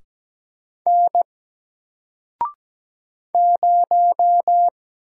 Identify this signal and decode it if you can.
Morse code, heard as a keyed tone, about 17 wpm, 700 Hz. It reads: N 0
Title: Single Letter-Number Alt 17wpm